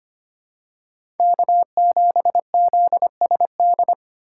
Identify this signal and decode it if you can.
Morse code, heard as a keyed tone, about 25 wpm, 700 Hz. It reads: K7ZSD
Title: USA Call Signs - 6x Character Spacing - 25wpm